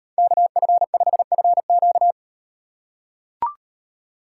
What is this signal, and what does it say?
Morse code, heard as a keyed tone, about 38 wpm, 700 Hz. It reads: KF5FQ